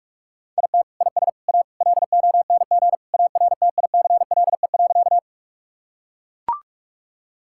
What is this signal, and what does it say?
Morse code, heard as a keyed tone, about 43 wpm, 700 Hz. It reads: IT IS A LONG ARTICLE.
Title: Sentences from Top 1000 Words - Review Part 5 - 45wpm